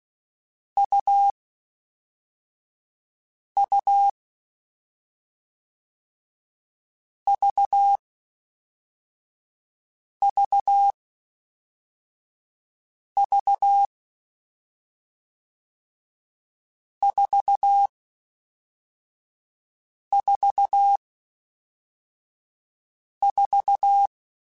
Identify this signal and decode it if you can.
Morse code, heard as a keyed tone, about 16 wpm, 800 Hz. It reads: U U V V V 4 4 4
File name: Difficult Combos / Difficult Combos-mp3s / AUV AUV4 @5